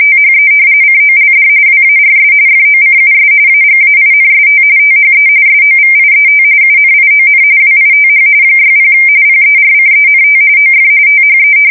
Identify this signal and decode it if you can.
RTTY, about 45.45 baud. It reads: THE COLD WARS FOCUS AGAINST THE SOVIET UNION WITH 60 PERCENT OF THE
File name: fsk170 / usnavy5